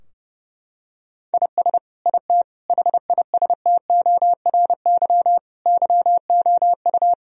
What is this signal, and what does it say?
Morse code, heard as a keyed tone, about 31 wpm, 700 Hz. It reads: IS IT HISTORY YOU
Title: Sentences from Top 1000 Words - Review Part 3 - 30wpm